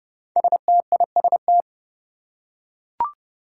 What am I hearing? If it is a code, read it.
Morse code, heard as a keyed tone, about 30 wpm, 700 Hz. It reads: STIST
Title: Five Challenging Letter-Number Combinations 30wpm